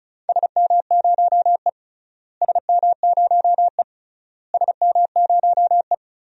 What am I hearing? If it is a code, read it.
Morse code, heard as a keyed tone, about 36 wpm, 700 Hz. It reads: SM0E SM0E SM0E
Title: DXCC-100 Call Signs - 4 Characters - Repeated 3x - 35wpm